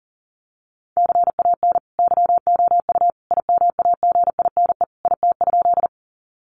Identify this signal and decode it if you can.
Morse code, heard as a keyed tone, about 40 wpm, 700 Hz. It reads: CAN YOU IMAGINE IT?
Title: Sentences from Top 1400 Words 40wpm